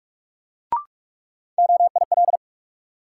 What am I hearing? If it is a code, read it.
Morse code, heard as a keyed tone, about 46 wpm, 700 Hz. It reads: OIL
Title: Sentences from Top 1000 Words - Review Part 4 - New Word Drill - 45wpm